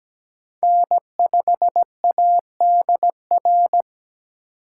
Morse code, heard as a keyed tone, about 17 wpm, 700 Hz. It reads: N5ADR
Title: USA Call Signs - 2x Character Spacing - 17wpm